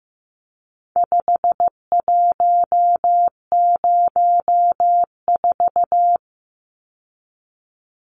Morse code, heard as a keyed tone, about 15 wpm, 700 Hz. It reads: 5104
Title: Numbers - 4-Digit 15wpm